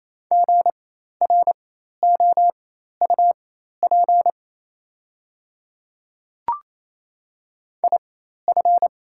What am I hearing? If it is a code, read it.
Morse code, heard as a keyed tone, about 28 wpm, spaced as for 13 wpm, 700 Hz. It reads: GROUP IF
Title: 4x Character Spacing - Top 300 Words - 28wpm